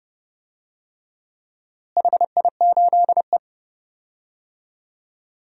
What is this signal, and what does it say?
Morse code, heard as a keyed tone, about 31 wpm, 700 Hz. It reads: HI8E
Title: DXCC-100 Call Signs - 4 Characters - Repeated 3x - 30wpm